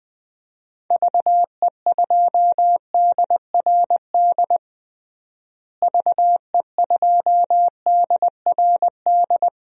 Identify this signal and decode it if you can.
Morse code, heard as a keyed tone, about 20 wpm, 700 Hz. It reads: VE2DRD VE2DRD
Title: DXCC-100 Call Signs - 6 Characters - Repeated 3x - 20wpm